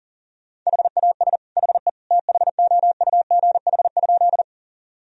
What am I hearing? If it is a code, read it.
Morse code, heard as a keyed tone, about 40 wpm, 700 Hz. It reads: HAS HE THOUGH?
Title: Sentences from Top 1000 Words - Review Part 2 - 40wpm